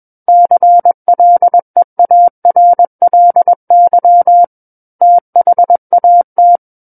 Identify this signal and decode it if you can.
Morse code, heard as a keyed tone, about 21 wpm, 700 Hz. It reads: CLEARLY THAT
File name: Book One / 12-Chapter 12 - 21 WPM